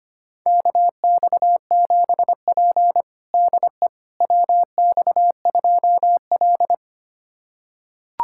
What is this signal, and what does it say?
Morse code, heard as a keyed tone, about 25 wpm, 700 Hz. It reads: KX7P DE WX2L
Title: USA Callsign DE Callsign - 4 Characters - 25wpm